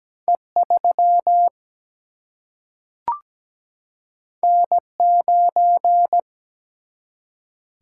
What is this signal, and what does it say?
Morse code, heard as a keyed tone, about 17 wpm, 700 Hz. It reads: E3 N9
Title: Single-Letter Single-Number 17wpm